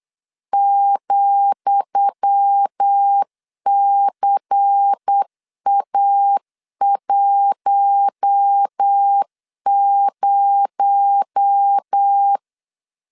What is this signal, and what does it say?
Morse code, heard as a keyed tone, about 8 wpm, 800 Hz. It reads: ,CA10
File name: LEZIONI / 10 SIGNS / 10D COMA 8wpm 3749